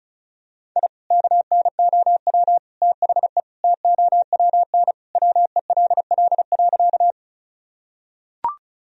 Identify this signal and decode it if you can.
Morse code, heard as a keyed tone, about 36 wpm, 700 Hz. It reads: I KNOW THE TOWN WELL.
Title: Sentences from Top 1000 Words - Review Part 2 - 35wpm